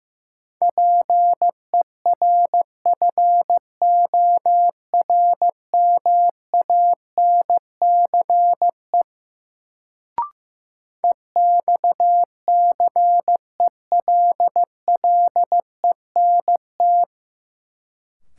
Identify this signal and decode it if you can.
Morse code, heard as a keyed tone, about 15 wpm, 700 Hz. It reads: PERFORMANCE EXCELLENT
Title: Sentences from Top 1000 Words - New Word Drill - 15wpm